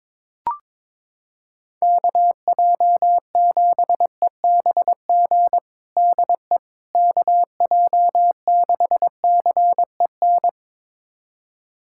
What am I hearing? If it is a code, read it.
Morse code, heard as a keyed tone, about 22 wpm, 700 Hz. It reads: KJ7EBG DE KJ6CEN
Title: USA Callsign DE Callsign - 6 Characters - 22wpm